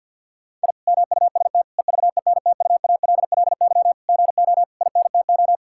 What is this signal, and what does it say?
Morse code, heard as a keyed tone, about 50 wpm, 700 Hz. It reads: I MUST EVENTUALLY GO INTO
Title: Sentences from Top 1000 Words - Review Part 6 - 50wpm